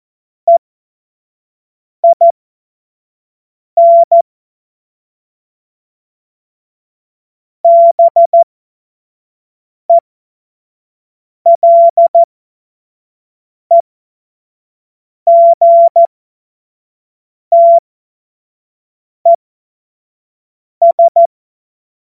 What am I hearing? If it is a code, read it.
Morse code, heard as a keyed tone, about 14 wpm, spaced as for 5 wpm, 680 Hz. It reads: EIN BELEGTES
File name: Morsekurs / Witze / Witz 21. Dezember 2021, 50 F70